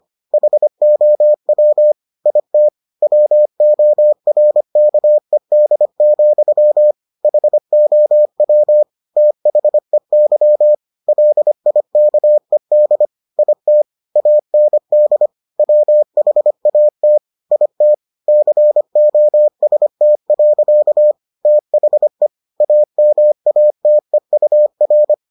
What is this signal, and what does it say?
Morse code, heard as a keyed tone, about 25 wpm, 600 Hz. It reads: HOW IT WORKED, HOW THEY LIKED IT AND WHAT IT COST. THE AMATEUR